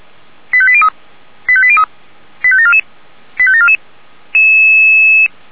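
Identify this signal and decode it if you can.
Five-tone selective call (ZVEI), heard as quick runs of five tones, tones of about 70 ms. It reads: 85803 85803 85744 85744
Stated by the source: ZVEI five-tone selective call received on an SDR.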